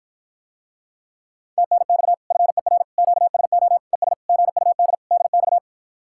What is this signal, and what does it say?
Morse code, heard as a keyed tone, about 54 wpm, 700 Hz. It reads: TNX FER QSO ES GUD DX